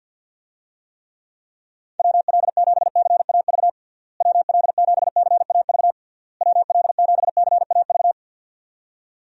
Morse code, heard as a keyed tone, about 52 wpm, 700 Hz. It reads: WL7CAV WL7CAV WL7CAV
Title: USA Call Signs - 6 Characters - Repeated 3x - 50wpm